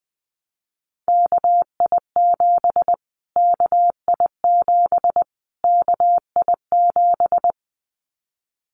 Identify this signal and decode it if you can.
Morse code, heard as a keyed tone, about 20 wpm, 700 Hz. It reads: KI7 KI7 KI7
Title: Callsign Prefixes for Top 25 DXCC - Repeated 3x - 20wpm